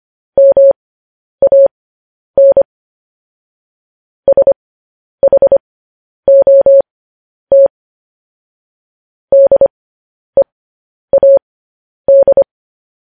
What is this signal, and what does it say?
Morse code, heard as a keyed tone, about 25 wpm, spaced as for 10 wpm, 550 Hz. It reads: MAN SHOT DEAD
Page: news headlines in morse code at 10 wpm